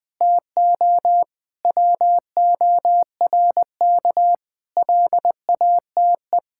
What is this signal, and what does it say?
Morse code, heard as a keyed tone, about 20 wpm, 700 Hz. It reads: TO WORK LATE
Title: Sentences with 5-Word Phrases - 20wpm